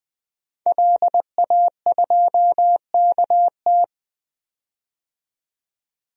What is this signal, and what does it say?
Morse code, heard as a keyed tone, about 20 wpm, 700 Hz. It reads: LA2KT
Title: DXCC-100 Call Signs - 5 Characters - 20wpm